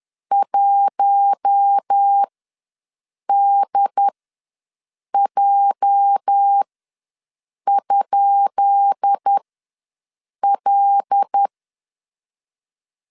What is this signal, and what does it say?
Morse code, heard as a keyed tone, about 10 wpm, spaced as for 6 wpm, 800 Hz. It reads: 1DJ?L
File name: LEZIONI / 10 SIGNS / 10B INT ws 10wpm 4134